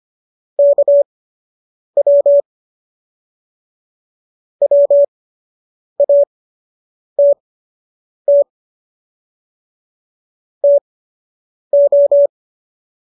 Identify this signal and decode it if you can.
Morse code, heard as a keyed tone, about 25 wpm, spaced as for 8 wpm, 570 Hz. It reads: KW WATT TO